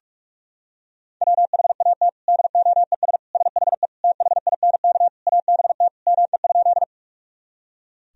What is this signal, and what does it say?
Morse code, heard as a keyed tone, about 41 wpm, 700 Hz. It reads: WHAT DOES SHE THINK ABT ME?
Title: Sentences from Top 200 Words and Abbreviations 45wpm